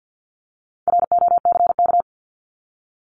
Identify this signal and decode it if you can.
Morse code, heard as a keyed tone, about 50 wpm, 700 Hz. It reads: ROCK
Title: Top 500 Words 50wpm